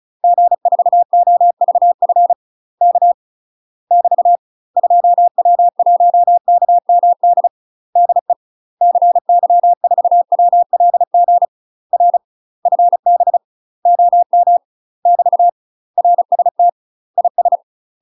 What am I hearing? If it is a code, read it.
Morse code, heard as a keyed tone, about 36 wpm, 700 Hz. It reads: G4OVF K <BT> 2W1KMD DE CY4WLG R FB OM <BT> RST IS
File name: UK MASTER QSOs TRAINING / 35WPM / UK MASTER QSOs 0010